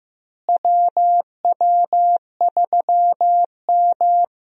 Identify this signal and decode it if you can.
Morse code, heard as a keyed tone, about 15 wpm, 700 Hz. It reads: WW3M
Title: USA Call Signs - 4 Characters - Repeated 2x - 15wpm